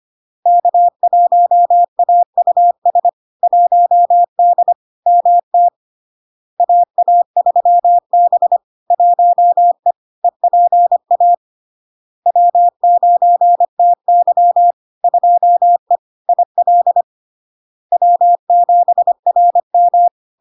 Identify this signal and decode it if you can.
Morse code, heard as a keyed tone, about 25 wpm, 700 Hz. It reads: K1AUS 1D MT AA3B 1E EPA W9TY 2E IL W7RM
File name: Log-in-CW-25wpm-700-Hz